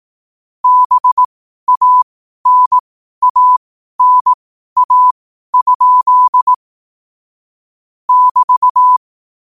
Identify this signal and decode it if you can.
Morse code, heard as a keyed tone, about 18 wpm, spaced as for 12 wpm, 1000 Hz. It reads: BANANA? <BT>
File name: qotdfiles / 12wpm